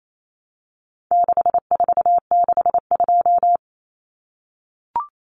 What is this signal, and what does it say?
Morse code, heard as a keyed tone, about 28 wpm, 700 Hz. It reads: 6462
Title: Numbers - 4-Digit 28wpm